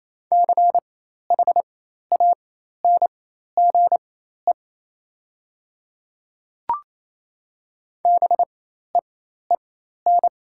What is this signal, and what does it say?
Morse code, heard as a keyed tone, about 28 wpm, spaced as for 13 wpm, 700 Hz. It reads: CHANGE BEEN
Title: 4x Character Spacing - Top 500 Words - 28wpm